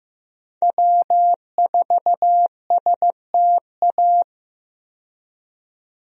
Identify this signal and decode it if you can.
Morse code, heard as a keyed tone, about 15 wpm, 700 Hz. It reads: W4STA
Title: USA Call Signs - 5 Characters - 15wpm